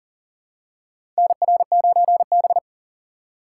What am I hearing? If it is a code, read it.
Morse code, heard as a keyed tone, about 40 wpm, 700 Hz. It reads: NR9B
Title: USA Call Signs - 3x Character Spacing - 40wpm